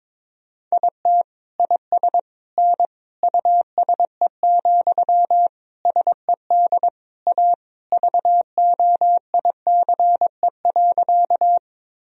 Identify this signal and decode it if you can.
Morse code, heard as a keyed tone, about 22 wpm, 700 Hz. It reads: IT IS N USE, SED A VOICE.
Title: Sentences from Top 500 Words and Abbreviations 22wpm